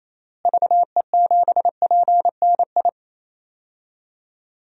Morse code, heard as a keyed tone, about 28 wpm, 700 Hz. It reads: VE7PNI
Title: DXCC-100 Call Signs - 6 Characters - Repeated 3x - 28wpm